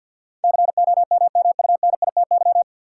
Morse code, heard as a keyed tone, about 49 wpm, 700 Hz. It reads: COMMUNITY
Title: Sentences from Top 1000 Words - Review Part 4 - New Word Drill - 50wpm